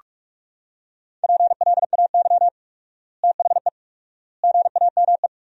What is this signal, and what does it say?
Morse code, heard as a keyed tone, about 42 wpm, 700 Hz. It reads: PLAY THE GAME
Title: 4x Word-Spacing - Sentences from Top 500 Words - 45wpm